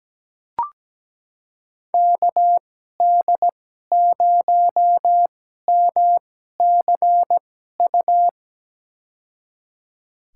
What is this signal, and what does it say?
Morse code, heard as a keyed tone, about 17 wpm, spaced as for 12 wpm, 700 Hz. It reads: KD0MCU
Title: USA Call Signs - 6 Characters - 2x Character Spacing - 17wpm